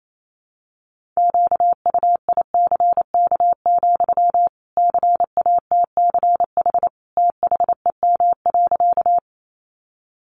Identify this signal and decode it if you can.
Morse code, heard as a keyed tone, about 28 wpm, 700 Hz. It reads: QUICK, CATCH THEM.
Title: Sentences from Top 1500 Words 28wpm